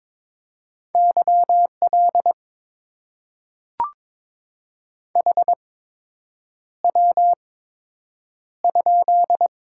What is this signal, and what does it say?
Morse code, heard as a keyed tone, about 22 wpm, spaced as for 8 wpm, 700 Hz. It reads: YL HW?
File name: Top 90 QSO Elements - 8x Character Spacing - 22wpm